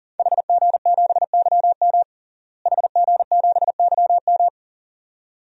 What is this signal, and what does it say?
Morse code, heard as a keyed tone, about 40 wpm, 700 Hz. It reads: HG7YM HG7YM
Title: DXCC-100 Call Signs - 5 Characters - Repeated 2x - 40wpm